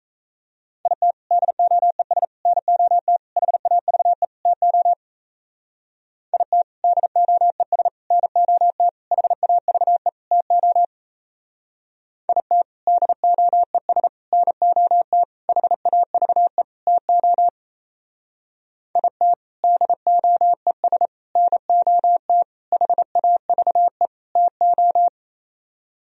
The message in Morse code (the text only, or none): IT DOES NOT HAVE TO IT DOES NOT HAVE TO IT DOES NOT HAVE TO IT DOES NOT HAVE TO